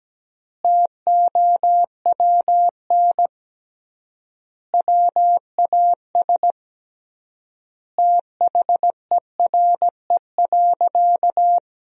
Morse code, heard as a keyed tone, about 17 wpm, 700 Hz. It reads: TOWN WAS THERE.
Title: 3x Word-Spacing - Sentences from Top 500 Words - 17wpm